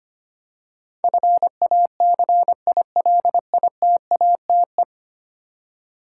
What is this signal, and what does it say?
Morse code, heard as a keyed tone, about 25 wpm, 700 Hz. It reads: FACILITATE
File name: 10 Letter Words - First 3 Letters Given - 25wpm